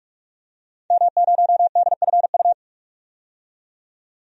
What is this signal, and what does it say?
Morse code, heard as a keyed tone, about 46 wpm, 700 Hz. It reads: M0DFU
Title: DXCC-100 Call Signs - 5 Characters - 45wpm